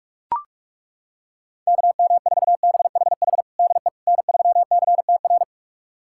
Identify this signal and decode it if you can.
Morse code, heard as a keyed tone, about 41 wpm, 700 Hz. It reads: KM4BHH DE N3CTR
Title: USA Callsign DE Callsign 45wpm